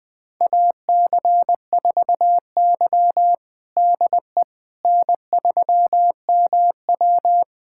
Morse code, heard as a keyed tone, about 20 wpm, 700 Hz. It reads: AC4Y DE N3MW
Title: USA Callsign DE Callsign - 4 Characters - 20wpm